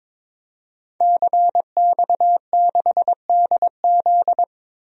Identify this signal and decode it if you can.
Morse code, heard as a keyed tone, about 22 wpm, 700 Hz. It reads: CX6DZ
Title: DXCC-100 Call Signs - 5 Characters - Repeated 3x - 22wpm